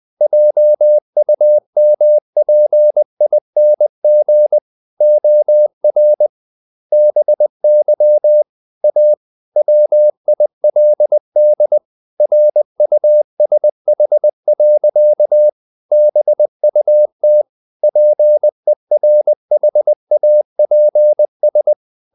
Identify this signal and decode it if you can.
Morse code, heard as a keyed tone, about 20 wpm, 600 Hz. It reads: JUMPING OR BY A WILD RUSH. BUT PERHAPS
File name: the old man and the sea / the old man and the sea part 1 20wpm